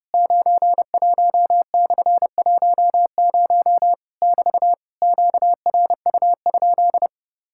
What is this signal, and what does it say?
Morse code, heard as a keyed tone, about 30 wpm, 700 Hz. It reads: 91/10 <BT> QRU?
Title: Worked All Europe DX Contest - QTC Exchange - 30wpm